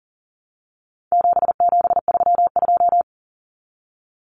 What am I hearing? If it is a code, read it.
Morse code, heard as a keyed tone, about 38 wpm, 700 Hz. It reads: 7732
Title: Common Cut Numbers - 4-Digit 40wpm